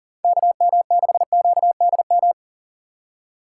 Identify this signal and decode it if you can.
Morse code, heard as a keyed tone, about 40 wpm, 700 Hz. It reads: KM6QDM
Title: USA Call Signs - 2x Character Spacing - 40wpm